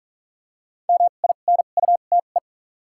Morse code, heard as a keyed tone, about 42 wpm, spaced as for 31 wpm, 700 Hz. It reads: MINUTE
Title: 2x Character Spacing - Top 500 Words - 45wpm